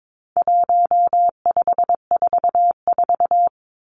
Morse code, heard as a keyed tone, about 22 wpm, 700 Hz. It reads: 1544
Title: Numbers - 4-Digit 22wpm